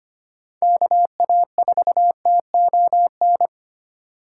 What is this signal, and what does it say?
Morse code, heard as a keyed tone, about 25 wpm, 700 Hz. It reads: KA4TON
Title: USA Call Signs - 3x Character Spacing - 25wpm